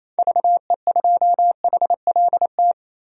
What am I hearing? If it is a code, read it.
Morse code, heard as a keyed tone, about 29 wpm, 700 Hz. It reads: VE2HLT
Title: DXCC-100 Call Signs - 6 Characters - 28wpm